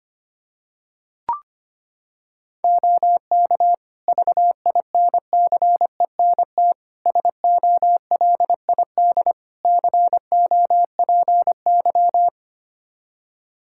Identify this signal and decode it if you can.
Morse code, heard as a keyed tone, about 25 wpm, 700 Hz. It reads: OK VINCENT SOLID COPY
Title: Practice QSO - Exchange 3 - More Info - 25wpm